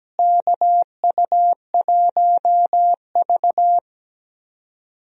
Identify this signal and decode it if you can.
Morse code, heard as a keyed tone, about 17 wpm, 700 Hz. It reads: KU1V
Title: USA Call Signs - 4 Characters - Repeated 3x - 17wpm